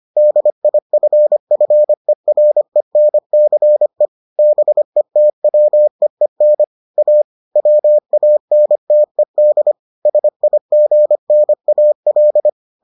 Morse code, heard as a keyed tone, about 25 wpm, 600 Hz. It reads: DIFFERENCE BETWEEN A WANTED SIGNAL